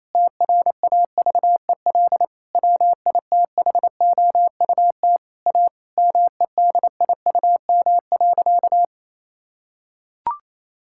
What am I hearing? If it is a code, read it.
Morse code, heard as a keyed tone, about 28 wpm, 700 Hz. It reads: TRAVEL WITHOUT A MEDIUM.
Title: Sentences from Top 1000 Words - Review Part 6 - 28wpm